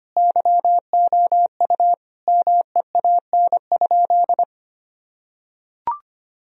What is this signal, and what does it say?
Morse code, heard as a keyed tone, about 25 wpm, 700 Hz. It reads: YOU MEAN?
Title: Sentences from Top 1000 Words - Review Part 3 - 25wpm